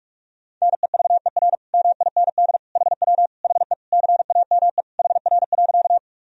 Morse code, heard as a keyed tone, about 46 wpm, 700 Hz. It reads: NEVER MIND HW HE CAME HR.
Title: Sentences from Top 500 Words and Abbreviations 45wpm